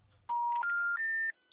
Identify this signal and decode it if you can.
Special information tone.